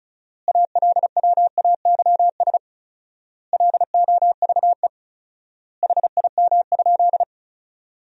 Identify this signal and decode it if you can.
Morse code, heard as a keyed tone, about 36 wpm, 700 Hz. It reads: ALWAYS LOVE HIM?
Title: 4x Word-Spacing - Sentences from Top 400 Words - 35wpm